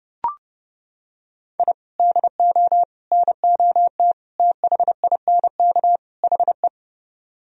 Morse code, heard as a keyed tone, about 29 wpm, 700 Hz. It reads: I DO NOT THINK HE